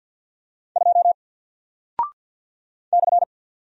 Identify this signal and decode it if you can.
Morse code, heard as a keyed tone, about 48 wpm, 700 Hz. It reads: 2 /
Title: Single Character (Letter:Number:Punctuation:Prosign) Alt 50wpm